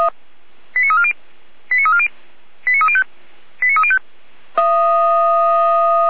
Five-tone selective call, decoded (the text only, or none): 89340 89340 89305 89305